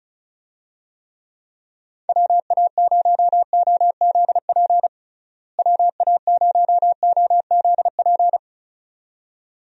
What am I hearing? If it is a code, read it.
Morse code, heard as a keyed tone, about 36 wpm, 700 Hz. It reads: WA0OZP WA0OZP
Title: USA Call Signs - 6 Characters - Repeated 2x - 35wpm